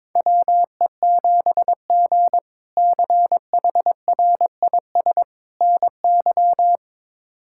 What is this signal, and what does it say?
Morse code, heard as a keyed tone, about 22 wpm, 700 Hz. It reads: WE7G CHRIS NY
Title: K1USN SST QSO Practice - 22wpm